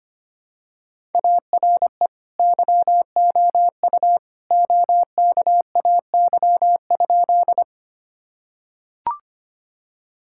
Morse code, heard as a keyed tone, about 25 wpm, 700 Hz. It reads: ARE YOU OKAY?
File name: practice / 25wpm / Sentences with 2-Word Phrases - 25wpm